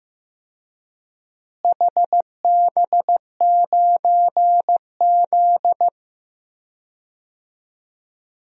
Morse code, heard as a keyed tone, about 15 wpm, 700 Hz. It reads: HB9Z
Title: DXCC-100 Call Signs - 4 Characters - Repeated 3x - 15wpm